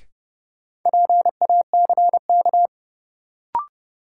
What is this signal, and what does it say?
Morse code, heard as a keyed tone, about 30 wpm, 700 Hz. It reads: PACK
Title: Sentences from Top 1300 Words - New Word Drill Alt - 30wpm